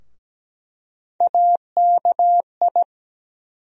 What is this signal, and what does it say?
Morse code, heard as a keyed tone, about 17 wpm, 700 Hz. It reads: AKI
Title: USA Call Sign Postfixes - 3 Letters 17wpm